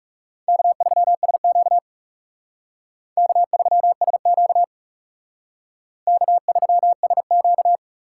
Morse code, heard as a keyed tone, about 40 wpm, 700 Hz. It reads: K3SQ K3SQ K3SQ